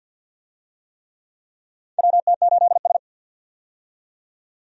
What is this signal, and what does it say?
Morse code, heard as a keyed tone, about 50 wpm, 700 Hz. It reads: WT8S
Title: USA Call Signs - 4 Characters - 50wpm